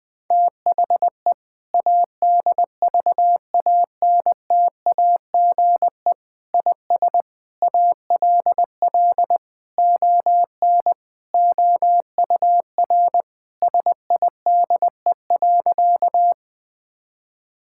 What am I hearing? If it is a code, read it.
Morse code, heard as a keyed tone, about 20 wpm, 700 Hz. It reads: THE ADVANTAGE IS ALL ON OUR SIDE.
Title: Sentences from Top 1000 Words - Review Part 6 - 20wpm